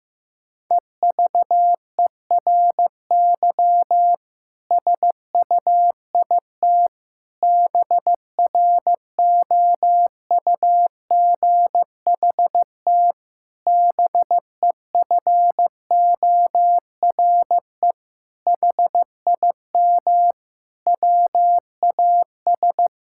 Morse code, heard as a keyed tone, about 16 wpm, 700 Hz. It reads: EVERY SUIT BROUGHT BEFORE HIM WAS
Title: Sentences from Top 1600 Words 15wpm